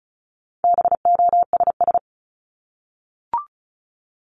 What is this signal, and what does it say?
Morse code, heard as a keyed tone, about 34 wpm, 700 Hz. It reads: BOSS
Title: Top 501-750 Words 35wpm